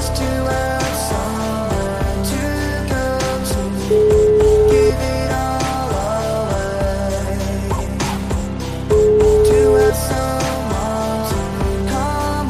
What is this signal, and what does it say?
Ringback tone (Europe, 425 Hz), with noise.